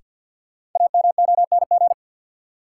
Morse code, heard as a keyed tone, about 53 wpm, 700 Hz. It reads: AMONG
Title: Sets of 2 Words - New Word Drill - 50wpm